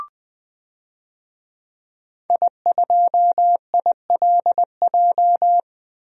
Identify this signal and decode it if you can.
Morse code, heard as a keyed tone, about 21 wpm, 700 Hz. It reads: I2ILJ
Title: DXCC-100 Call Signs - 5 Characters - Repeated 2x - 20wpm